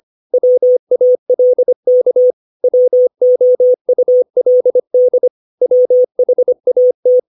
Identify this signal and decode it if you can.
Morse code, heard as a keyed tone, about 25 wpm, 500 Hz. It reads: WALK WOULD WHAT
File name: 632 words / 632 Words 25WPM